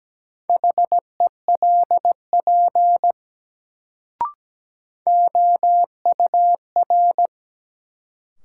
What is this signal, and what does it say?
Morse code, heard as a keyed tone, about 17 wpm, 700 Hz. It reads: HELP OUR
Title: Sentences from Top 200 Words - New Word Drill - 17wpm